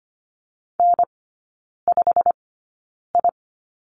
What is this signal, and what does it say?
Morse code, heard as a keyed tone, about 25 wpm, spaced as for 9 wpm, 700 Hz. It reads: N5I
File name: Lesson 15e - Number 5 - Three Characters 25wpm